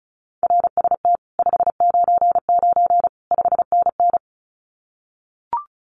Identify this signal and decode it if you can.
Morse code, heard as a keyed tone, about 36 wpm, 700 Hz. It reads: RST 599 5NN
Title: Common QSO Phrases Encore 35wpm